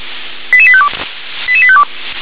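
Five-tone selective call (ZVEI), heard as quick runs of five tones, tones of about 70 ms. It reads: 88852 88852
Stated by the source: ZVEI five-tone selective call received by SDR.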